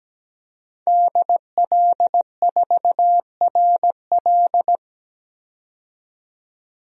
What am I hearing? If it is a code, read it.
Morse code, heard as a keyed tone, about 17 wpm, 700 Hz. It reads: DL4RL
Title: DXCC-100 Call Signs - 5 Characters - 17wpm